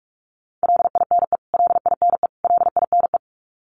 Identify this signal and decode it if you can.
Morse code, heard as a keyed tone, about 44 wpm, 700 Hz. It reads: LINE LINE LINE
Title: Top 200 Words Repeated 45wpm